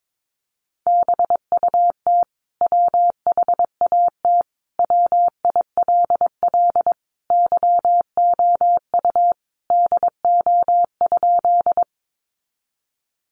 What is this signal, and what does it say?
Morse code, heard as a keyed tone, about 22 wpm, 700 Hz. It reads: BUT WHAT WILL YOU DO?
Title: Sentences from Top 100 Words 22wpm